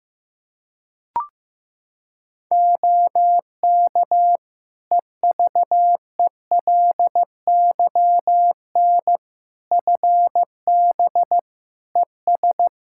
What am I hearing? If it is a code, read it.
Morse code, heard as a keyed tone, about 15 wpm, 700 Hz. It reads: OK EVELYN FB ES
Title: Practice QSO - Exchange 2 - Info - 15wpm